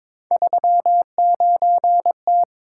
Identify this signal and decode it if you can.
Morse code, heard as a keyed tone, about 22 wpm, 700 Hz. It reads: 39T
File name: practice / 22wpm / Three Letter-Number Combinations - 22wpm